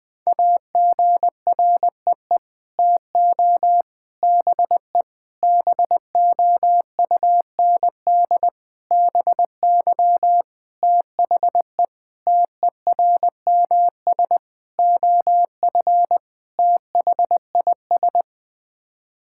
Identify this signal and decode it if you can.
Morse code, heard as a keyed tone, about 20 wpm, 700 Hz. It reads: AGREE TO BE BOUND BY THE TERMS OF THIS
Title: Sets of 9 Words - 20wpm